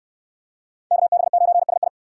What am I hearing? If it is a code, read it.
Morse code, heard as a keyed tone, about 69 wpm, 700 Hz. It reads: KB9SI